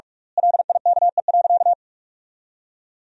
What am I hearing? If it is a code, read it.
Morse code, heard as a keyed tone, about 46 wpm, 700 Hz. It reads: LIKE.
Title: 2x Word-Spacing - Sentences from Top 500 Words - 45wpm